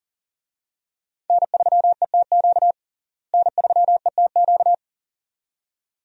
Morse code, heard as a keyed tone, about 40 wpm, 700 Hz. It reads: N3ETQ N3ETQ
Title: USA Call Signs - 5 Characters - Repeated 2x - 40wpm